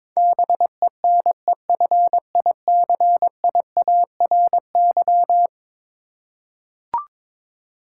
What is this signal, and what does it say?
Morse code, heard as a keyed tone, about 22 wpm, 700 Hz. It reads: BENEFICIARY
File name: practice / 22wpm / Eleven Letter Words - 22wpm